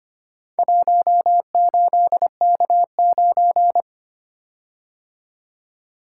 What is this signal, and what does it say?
Morse code, heard as a keyed tone, about 25 wpm, 690 Hz. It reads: 18K9
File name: practice / 25wpm / Four Letter-Number Combinations - 25wpm